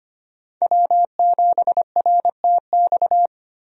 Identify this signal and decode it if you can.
Morse code, heard as a keyed tone, about 25 wpm, 700 Hz. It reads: W7RTX
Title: USA Call Signs - 5 Characters - 25wpm